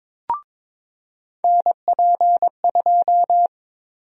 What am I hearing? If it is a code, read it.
Morse code, heard as a keyed tone, about 21 wpm, 700 Hz. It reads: NP2
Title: USA Call Sign Prefixes - 22wpm